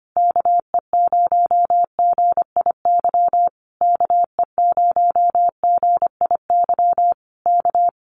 Morse code, heard as a keyed tone, about 25 wpm, 700 Hz. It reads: KE0GIY KE0GIY K
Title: Common QSO Phrases Encore 25wpm